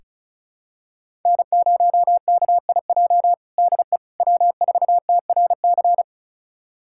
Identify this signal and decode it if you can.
Morse code, heard as a keyed tone, about 36 wpm, 700 Hz. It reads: N0KIJ DE W4TRC
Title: USA Callsign DE Callsign - 5 Characters - 35wpm